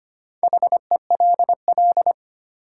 Morse code, heard as a keyed tone, about 25 wpm, 700 Hz. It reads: HELL
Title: Sets of 5 Words Encore - New Word Drill - 25wpm